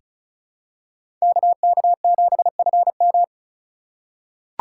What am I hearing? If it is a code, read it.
Morse code, heard as a keyed tone, about 34 wpm, 700 Hz. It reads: KK7FM
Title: USA Call Signs - 5 Characters - 4x Character Spacing - 35wpm